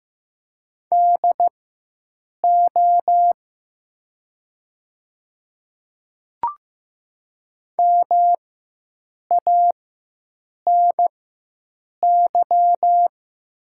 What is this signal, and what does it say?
Morse code, heard as a keyed tone, about 15 wpm, spaced as for 7 wpm, 700 Hz. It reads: DO MANY
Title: 4x Character Spacing - Top 200 Words - 15wpm